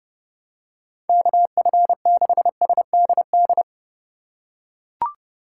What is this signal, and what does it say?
Morse code, heard as a keyed tone, about 30 wpm, 700 Hz. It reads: KF6SDD
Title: USA Call Signs - 6 Characters - 2x Character Spacing - 30wpm